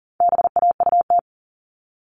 Morse code, heard as a keyed tone, about 40 wpm, 700 Hz. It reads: BAUT
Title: All Cut Numbers - 4-Digit 40wpm